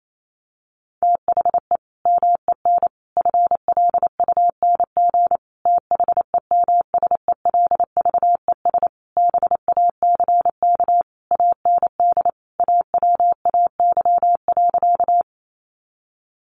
Morse code, heard as a keyed tone, about 28 wpm, 700 Hz. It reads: THE MEN FLUNG THEMSELVES BACK AND AWAY.
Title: A Sound Of Thunder 28wpm